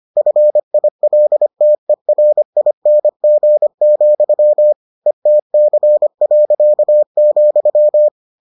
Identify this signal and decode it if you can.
Morse code, heard as a keyed tone, about 25 wpm, 600 Hz. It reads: FILTERING, ETC.,